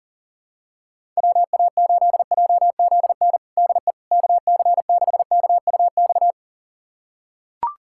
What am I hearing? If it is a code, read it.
Morse code, heard as a keyed tone, about 40 wpm, 700 Hz. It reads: WA8JZN DE KC6KUX
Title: USA Callsign DE Callsign - 6 Characters - 40wpm